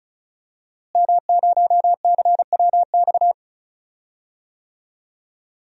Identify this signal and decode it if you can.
Morse code, heard as a keyed tone, about 36 wpm, 700 Hz. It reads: M0CWX
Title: DXCC-100 Call Signs - 5 Characters - Repeated 3x - 35wpm